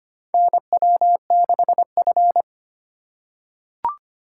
Morse code, heard as a keyed tone, about 25 wpm, 700 Hz. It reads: NW6F
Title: USA Call Signs - 4 Characters - 2x Character Spacing - 25wpm